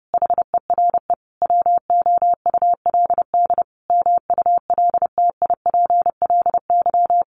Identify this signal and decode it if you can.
Morse code, heard as a keyed tone, about 30 wpm, 700 Hz. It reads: HERE WOULD MULTIPLY
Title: A Sound Of Thunder 30wpm